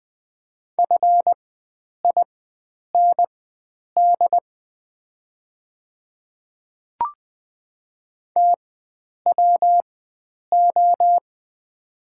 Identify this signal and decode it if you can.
Morse code, heard as a keyed tone, about 20 wpm, spaced as for 9 wpm, 690 Hz. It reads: FIND TWO
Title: 4x Character Spacing - Top 300 Words - 20wpm